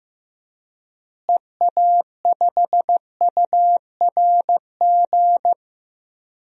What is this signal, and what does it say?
Morse code, heard as a keyed tone, about 15 wpm, 700 Hz. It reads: EA5URG
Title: DXCC-100 Call Signs - 6 Characters - Repeated 3x - 15wpm